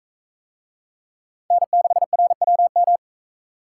Morse code, heard as a keyed tone, about 42 wpm, 700 Hz. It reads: N6RWM